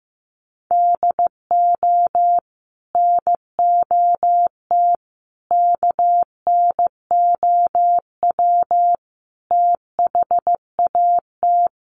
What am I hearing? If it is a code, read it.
Morse code, heard as a keyed tone, about 15 wpm, 700 Hz. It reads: DO NOT KNOW THAT
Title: Sets of 5 Words Encore2 15wpm